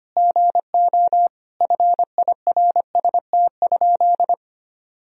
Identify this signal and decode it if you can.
Morse code, heard as a keyed tone, about 25 wpm, 700 Hz. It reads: GO FIRST?
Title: Sentences from Top 1000 Words - Review Part 3 - 25wpm